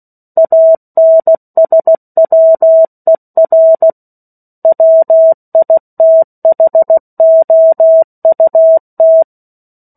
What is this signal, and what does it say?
Morse code, heard as a keyed tone, about 16 wpm, 650 Hz. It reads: ANSWER WITHOUT
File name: 03-Chapter-03 16 WPM